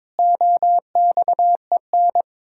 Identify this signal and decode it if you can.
Morse code, heard as a keyed tone, about 22 wpm, 700 Hz. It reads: OXEN
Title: Pangram Challenge v2 - 22wpm